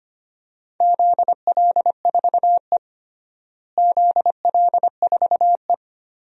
Morse code, heard as a keyed tone, about 25 wpm, 700 Hz. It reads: ZL4E ZL4E
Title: DXCC-100 Call Signs - 4 Characters - Repeated 2x - 25wpm